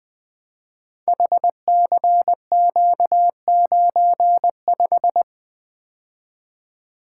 Morse code, heard as a keyed tone, about 21 wpm, 700 Hz. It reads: HCQ95
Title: Code Groups - 5 Letter:Numbers - 20wpm